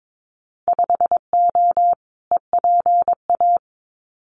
Morse code, heard as a keyed tone, about 22 wpm, 700 Hz. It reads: 5O EPA
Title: Winter Field Day Exchanges Encore 22wpm